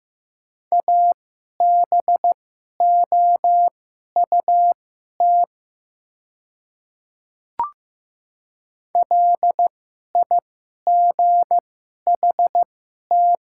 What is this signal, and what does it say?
Morse code, heard as a keyed tone, about 15 wpm, spaced as for 11 wpm, 700 Hz. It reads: ABOUT LIGHT
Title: 2x Character Spacing - Top 300 Words - 15wpm